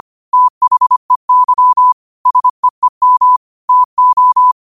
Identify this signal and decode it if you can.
Morse code, heard as a keyed tone, about 25 wpm, 1000 Hz. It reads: THEY SEEM TO